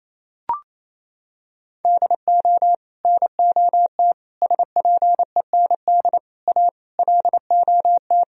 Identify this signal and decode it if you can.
Morse code, heard as a keyed tone, about 28 wpm, 700 Hz. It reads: DO NOT SPEND A LOT
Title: Sentences from Top 1000 Words - Review Part 4 - 28wpm